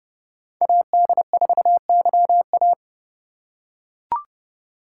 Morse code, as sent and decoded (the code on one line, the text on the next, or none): .- -.. ....- -.-- .-
AD4YA